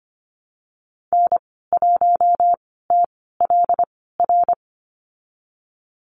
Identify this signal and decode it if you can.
Morse code, heard as a keyed tone, about 25 wpm, spaced as for 16 wpm, 700 Hz. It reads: N1TLR
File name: Lesson 15g - Number 5 - Call Signs 25wpm